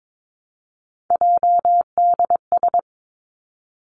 Morse code, heard as a keyed tone, about 22 wpm, 700 Hz. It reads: JDS
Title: Three Challenging Letter-Number Combinations 22wpm